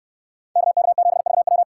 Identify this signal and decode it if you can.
Morse code, heard as a keyed tone, about 69 wpm, 700 Hz. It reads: KK7VW